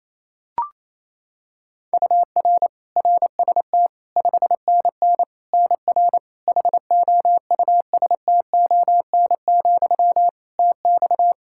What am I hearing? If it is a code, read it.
Morse code, heard as a keyed tone, about 27 wpm, 700 Hz. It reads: UR RST 5NN NR HOUSTON, TX
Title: K1USN SST QSO Practice 28wpm